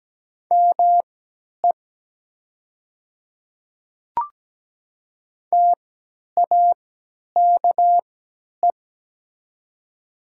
Morse code, heard as a keyed tone, about 17 wpm, spaced as for 10 wpm, 700 Hz. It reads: ME TAKE
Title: 3x Character Spacing - Top 500 Words - 17wpm